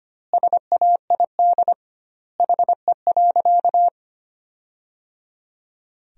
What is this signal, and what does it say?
Morse code, heard as a keyed tone, about 25 wpm, 700 Hz. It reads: SAID HE.
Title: 2x Word-Spacing - Sentences from Top 400 Words - 25wpm